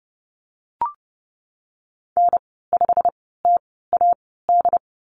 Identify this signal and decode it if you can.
Morse code, heard as a keyed tone, about 30 wpm, spaced as for 17 wpm, 700 Hz. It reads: N5TAD
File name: Lesson 19g - Letter w - Call Signs 30wpm